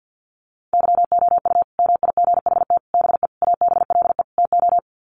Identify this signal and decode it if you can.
Morse code, heard as a keyed tone, about 50 wpm, 700 Hz. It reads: YOU MIGHT BE ABLE TO
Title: Sets of 5 Words Encore2 50wpm